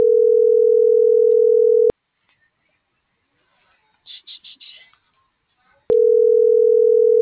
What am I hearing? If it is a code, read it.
Ringback tone (North America), 440 + 480 Hz.